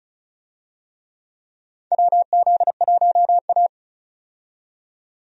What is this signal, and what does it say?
Morse code, heard as a keyed tone, about 34 wpm, 700 Hz. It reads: WZ1A
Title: USA Call Signs - 4 Characters - 35wpm